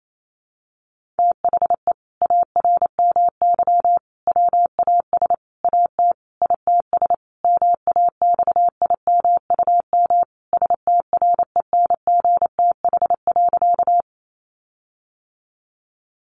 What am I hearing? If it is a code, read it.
Morse code, heard as a keyed tone, about 28 wpm, 700 Hz. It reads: THE ARMY WAS AT ITS MAXIMUM STRENGTH.
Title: Sentences from Top 1300 Words 28wpm